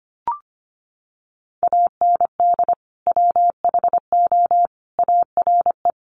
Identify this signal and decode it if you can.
Morse code, heard as a keyed tone, about 25 wpm, 700 Hz. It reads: AND WHO ARE
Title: Sentences from Top 100 Words 25wpm